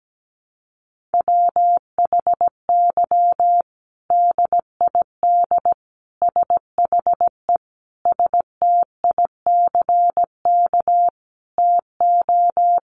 Morse code, heard as a keyed tone, about 17 wpm, 700 Hz. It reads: WHY DID SHE STICK TO
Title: Sentences from Top 1200 Words 17wpm